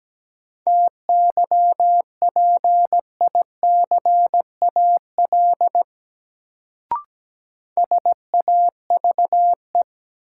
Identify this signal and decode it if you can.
Morse code, heard as a keyed tone, about 17 wpm, 700 Hz. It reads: TYPICAL SAVE
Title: Sentences from Top 1100 Words - New Word Drill Alt - 17wpm